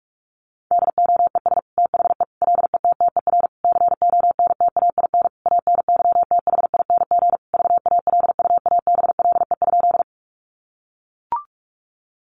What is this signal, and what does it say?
Morse code, heard as a keyed tone, about 44 wpm, 700 Hz. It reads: DOES THE LETTER CONTAIN ANYTHING VALUABLE?
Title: Sentences from Top 1100 Words 45wpm